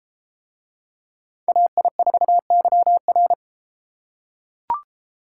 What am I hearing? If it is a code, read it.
Morse code, heard as a keyed tone, about 29 wpm, 700 Hz. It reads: AI4YR